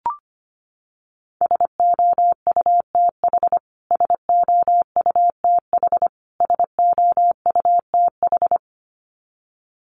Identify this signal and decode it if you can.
Morse code, heard as a keyed tone, about 25 wpm, 700 Hz. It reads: SOUTH SOUTH SOUTH
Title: Top 500 Words Repeated 25wpm